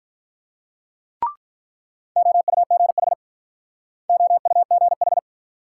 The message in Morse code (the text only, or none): OUGH OUGH